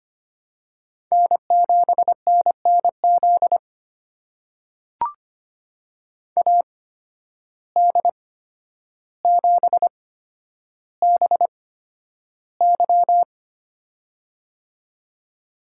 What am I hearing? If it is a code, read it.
Morse code, heard as a keyed tone, about 25 wpm, spaced as for 11 wpm, 700 Hz. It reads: N7NNZ AD7BY
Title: USA Call Signs - 5 Characters - 8x Character Spacing - 25wpm